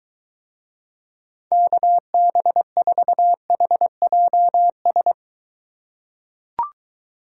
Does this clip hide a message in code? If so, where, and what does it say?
Yes: Morse code from 1.5 to 5.1 s, heard as a keyed tone, about 23 wpm, 700 Hz. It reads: KB4HJS